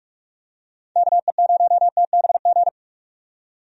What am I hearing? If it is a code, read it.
Morse code, heard as a keyed tone, about 46 wpm, 700 Hz. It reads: KE0TBG
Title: USA Call Signs - 6 Characters - 45wpm